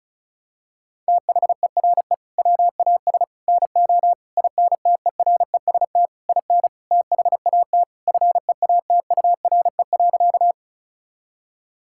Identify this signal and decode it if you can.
Morse code, heard as a keyed tone, about 36 wpm, 700 Hz. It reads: THERE WAS NO INTEREST IN THAT FEATURE.
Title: Sentences from Top 1000 Words - Review Part 5 - 35wpm